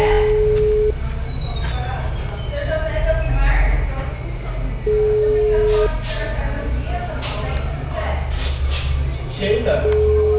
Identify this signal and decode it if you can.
Ringback tone (Europe, 425 Hz), with speech.
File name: out- -2015- - - .28489